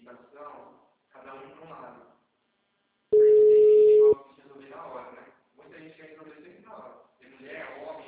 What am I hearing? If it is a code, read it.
Ringback tone (Europe, 425 Hz), with speech.